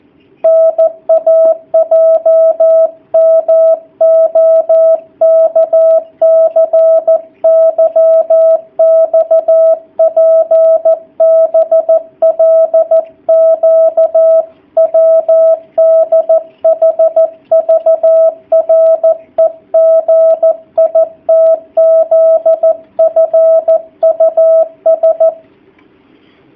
Morse code, heard as a keyed tone, about 14 wpm, 650 Hz, with radio noise. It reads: NAJMOKCYXPBLQWDHVREGITZFUS
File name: jh8cbh / 2019cw / sp13- w